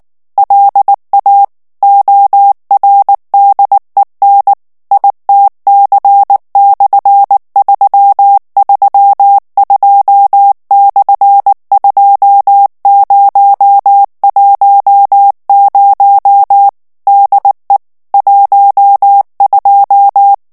Morse code, heard as a keyed tone, about 19 wpm, 800 Hz. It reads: LA ORDEN ITC/332/2010 DE 12
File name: morse mp3 / ESPAGNOL TEXTE EN CLAIR 2 VITESSE 18 MOTS MINUTES